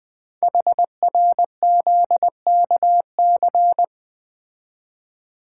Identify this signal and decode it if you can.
Morse code, heard as a keyed tone, about 20 wpm, 700 Hz. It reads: HRZKC
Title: Code Groups - 5 Letters - 20wpm